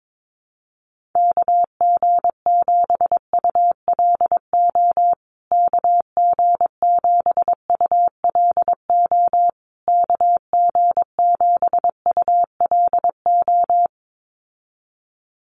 Morse code, heard as a keyed tone, about 22 wpm, 700 Hz. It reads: KG7ULO KG7ULO KG7ULO
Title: USA Call Signs Repeated 3x 22wpm